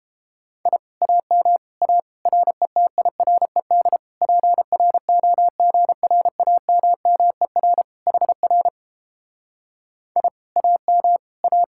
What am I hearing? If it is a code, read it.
Morse code, heard as a keyed tone, about 31 wpm, 700 Hz. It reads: I AM A RETIRED PROGRAMMER HR I AM A